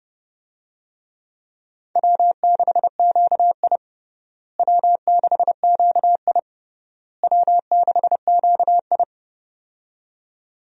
Morse code, heard as a keyed tone, about 30 wpm, 700 Hz. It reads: W6QI W6QI W6QI
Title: USA Call Signs - 4 Characters - Repeated 3x - 30wpm